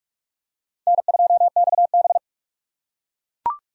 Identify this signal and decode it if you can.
Morse code, heard as a keyed tone, about 44 wpm, 700 Hz. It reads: N2XB
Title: USA Call Signs - 4 Characters - 4x Character Spacing - 45wpm